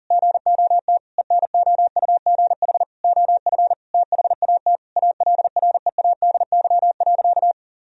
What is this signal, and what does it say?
Morse code, heard as a keyed tone, about 37 wpm, 700 Hz. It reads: GOT ENOUGH OF THAT ALREADY.
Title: Sentences from Top 1000 Words - Review Part 3 - 40wpm